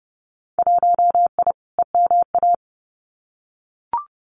Morse code, heard as a keyed tone, about 30 wpm, 700 Hz. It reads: 1I EMA
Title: Winter Field Day Exchanges Encore 30wpm